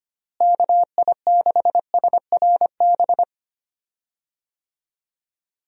Morse code, heard as a keyed tone, about 25 wpm, 700 Hz. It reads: KI6SRB
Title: USA Call Signs - 6 Characters - Repeated 3x - 25wpm